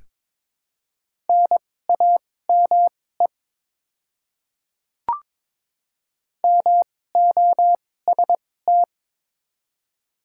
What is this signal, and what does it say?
Morse code, heard as a keyed tone, about 22 wpm, spaced as for 16 wpm, 700 Hz. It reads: NAME MOST
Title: 2x Character Spacing - Top 200 Words - 22wpm